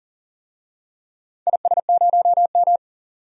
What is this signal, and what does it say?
Morse code, heard as a keyed tone, about 40 wpm, 700 Hz. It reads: IS0M
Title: DXCC-100 Call Signs - 4 Characters - 40wpm